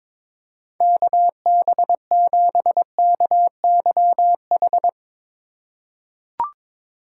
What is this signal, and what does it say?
Morse code, heard as a keyed tone, about 22 wpm, 700 Hz. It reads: KB7KYH
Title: USA Call Signs - 3x Character Spacing - 22wpm